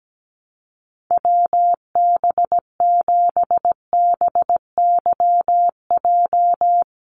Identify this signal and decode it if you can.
Morse code, heard as a keyed tone, about 17 wpm, 700 Hz. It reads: WB7BYJ
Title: USA Call Signs 17wpm